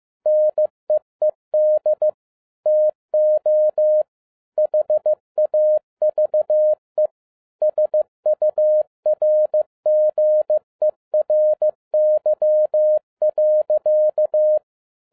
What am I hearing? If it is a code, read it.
Morse code, heard as a keyed tone, about 15 wpm, 610 Hz. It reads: NEED TO HAVE SURGERY.
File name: 2011-07-mp3 / n57 BACK FROM A BIG ACCIDENT 15wpm